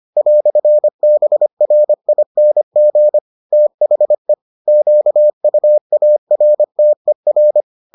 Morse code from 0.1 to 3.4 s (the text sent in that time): "BRING